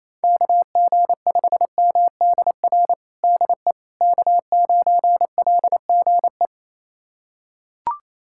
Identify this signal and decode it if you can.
Morse code, heard as a keyed tone, about 28 wpm, 700 Hz. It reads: KG5MDR DE K9LGE
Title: USA Callsign DE Callsign 28wpm